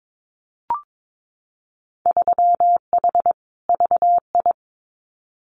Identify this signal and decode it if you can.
Morse code, heard as a keyed tone, about 22 wpm, 700 Hz. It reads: 3H VI
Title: Winter Field Day Exchanges Encore 22wpm